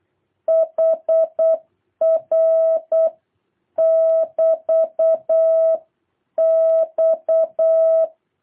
Morse code, heard as a keyed tone, about 8 wpm, 660 Hz. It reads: HR <BT> X
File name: jh8cbh / 2023cw